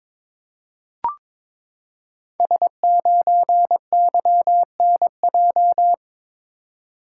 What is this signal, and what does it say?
Morse code, heard as a keyed tone, about 22 wpm, 700 Hz. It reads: S9YNJ
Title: Code Groups - 5 Letter:Numbers - 22wpm